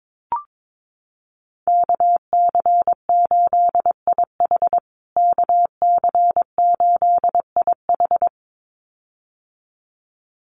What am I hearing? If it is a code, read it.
Morse code, heard as a keyed tone, about 22 wpm, 700 Hz. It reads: KC8IH KC8IH
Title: USA Call Signs Repeated 2x 22wpm